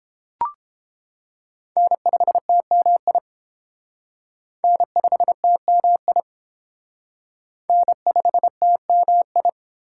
Morse code, heard as a keyed tone, about 30 wpm, 700 Hz. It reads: N5TMI N5TMI N5TMI